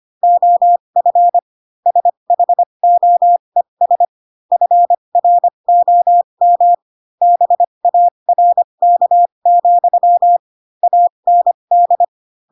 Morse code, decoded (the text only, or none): OF SHOES FROM BARK, AND